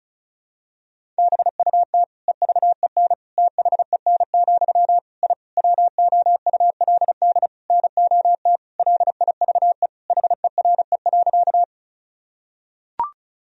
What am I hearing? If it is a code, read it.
Morse code, heard as a keyed tone, about 34 wpm, 700 Hz. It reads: BUT EVEN THEN, I WOULD NOT LIVE HERE.
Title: Sentences from Top 1000 Words - Review Part 1 - 35wpm